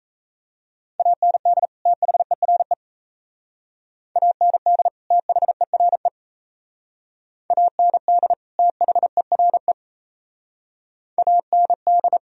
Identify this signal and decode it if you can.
Morse code, heard as a keyed tone, about 33 wpm, 700 Hz. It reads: AND THERE AND THERE AND THERE AND
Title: Speed-Racing - Sets of 2 Words 28wpm